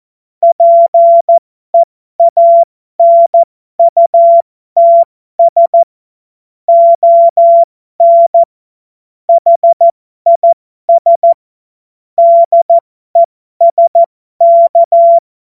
Morse code, heard as a keyed tone, about 14 wpm, spaced as for 12 wpm, 680 Hz. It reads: PEANUTS ON HIS DESK
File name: Morsekurs / Witze